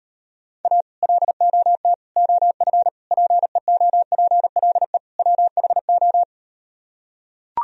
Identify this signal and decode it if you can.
Morse code, heard as a keyed tone, about 39 wpm, 700 Hz. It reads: A LOT OF PEOPLE WHO